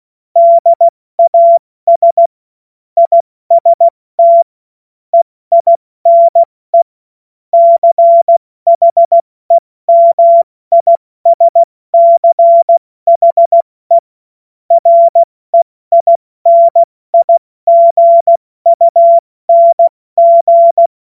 Morse code, heard as a keyed tone, about 16 wpm, 680 Hz. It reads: DAS IST EINE CHEMISCHE REINIGUNG
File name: Witz-30-September-2025-70-F80